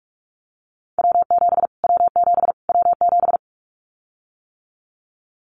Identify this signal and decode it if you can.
Morse code, heard as a keyed tone, about 44 wpm, 700 Hz. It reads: W7 W7 W7
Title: Callsign Prefixes for Top 25 DXCC - Repeated 3x - 45wpm